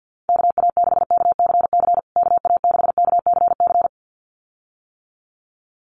Morse code, heard as a keyed tone, about 50 wpm, 700 Hz. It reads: KA6KCC KA6KCC
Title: USA Call Signs Repeated 2x 50wpm